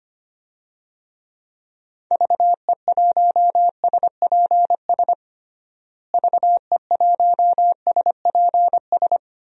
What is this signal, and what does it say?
Morse code, heard as a keyed tone, about 25 wpm, 700 Hz. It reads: VE1SPS VE1SPS
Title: DXCC-100 Call Signs - 6 Characters - Repeated 2x - 25wpm